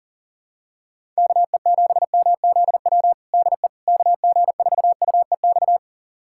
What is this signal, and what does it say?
Morse code, heard as a keyed tone, about 37 wpm, 700 Hz. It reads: KE7MZW DE KG4UEX
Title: USA Callsign DE Callsign - 6 Characters - 40wpm